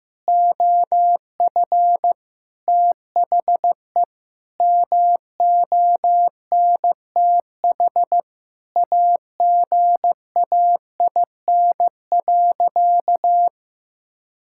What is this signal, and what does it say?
Morse code, heard as a keyed tone, about 15 wpm, 700 Hz. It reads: OF THE MONTH AGAIN.
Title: Sentences from Top 1000 Words - Review Part 5 - 15wpm